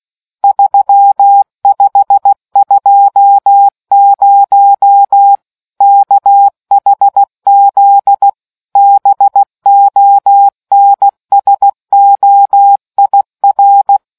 Morse code, heard as a keyed tone, about 16 wpm, 800 Hz. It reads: 3520 KHZ BONSOIR